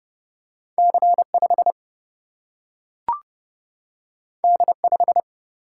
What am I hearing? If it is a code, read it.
Morse code, heard as a keyed tone, about 30 wpm, 700 Hz. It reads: C5 D5
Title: Single-Letter Single-Number 30wpm